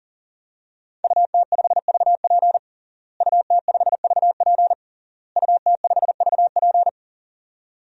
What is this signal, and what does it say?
Morse code, heard as a keyed tone, about 40 wpm, 700 Hz. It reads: UT5VP UT5VP UT5VP
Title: DXCC-100 Call Signs - 5 Characters - Repeated 3x - 40wpm